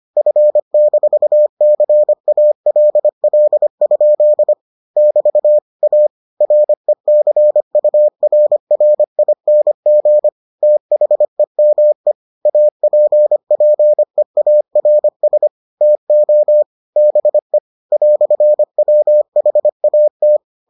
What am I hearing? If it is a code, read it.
Morse code, heard as a keyed tone, about 25 wpm, 600 Hz. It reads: F-CALL? <BT> A RECURRING THEME APPEARS TO BE "WHAT